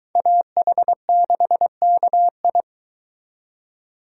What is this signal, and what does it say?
Morse code, heard as a keyed tone, about 23 wpm, 700 Hz. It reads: AH6KI